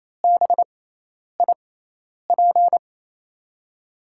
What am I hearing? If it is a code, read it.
Morse code, heard as a keyed tone, about 28 wpm, spaced as for 10 wpm, 700 Hz. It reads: BIP
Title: USA Call Signs - 6 Characters - 6x Character Spacing - 28wpm